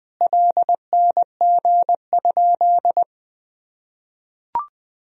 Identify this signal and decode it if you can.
Morse code, heard as a keyed tone, about 21 wpm, 700 Hz. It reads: LNG?
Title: Sentences from Top 200 Words and Abbreviations 20wpm